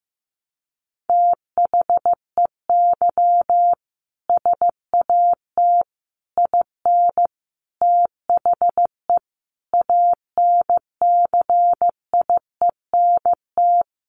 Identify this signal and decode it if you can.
Morse code, heard as a keyed tone, about 15 wpm, 700 Hz. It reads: THEY SAT IN THE ANCIENT
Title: A Sound Of Thunder 15wpm